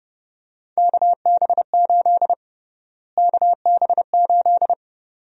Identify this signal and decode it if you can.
Morse code, heard as a keyed tone, about 31 wpm, 700 Hz. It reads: KB8 KB8
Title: USA Call Sign Prefixes - Repeated 2x - 30wpm